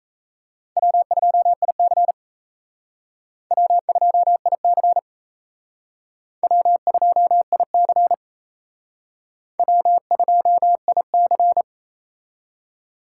Morse code, heard as a keyed tone, about 37 wpm, 700 Hz. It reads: W2IC W2IC W2IC W2IC